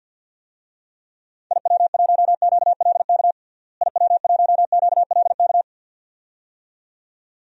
Morse code, heard as a keyed tone, about 47 wpm, 700 Hz. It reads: IW1QLK IW1QLK
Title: DXCC-100 Call Signs - 6 Characters - Repeated 2x - 50wpm